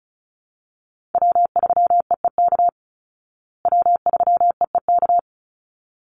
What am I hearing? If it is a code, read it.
Morse code, heard as a keyed tone, about 34 wpm, 700 Hz. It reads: W3EEK W3EEK
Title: Call Signs Repeated 3x - 35wpm